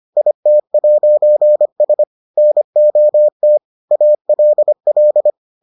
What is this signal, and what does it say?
Morse code, heard as a keyed tone, about 25 wpm, 600 Hz. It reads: IT'S NOT ALL